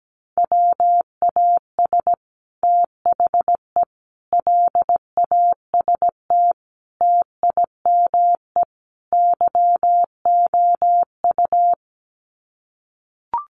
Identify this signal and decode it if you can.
Morse code, heard as a keyed tone, about 17 wpm, 700 Hz. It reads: WAS THE LAST TIME YOU
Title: Sets of 5 Words Encore2 17wpm